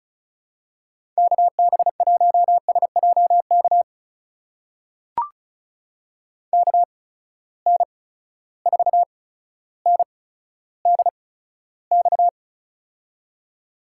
Morse code, heard as a keyed tone, about 36 wpm, spaced as for 15 wpm, 700 Hz. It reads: KB1SJK KN4NDX
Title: USA Call Signs - 6 Characters - 8x Character Spacing - 35wpm